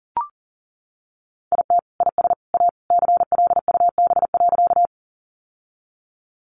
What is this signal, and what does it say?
Morse code, heard as a keyed tone, about 40 wpm, 700 Hz. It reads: IT IS A CLUB.
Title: Sentences from Top 1200 Words 40wpm